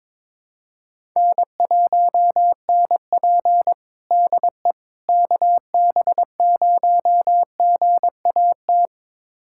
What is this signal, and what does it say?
Morse code, heard as a keyed tone, about 22 wpm, 700 Hz. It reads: N1NP DE KB0GAT
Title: USA Callsign DE Callsign 22wpm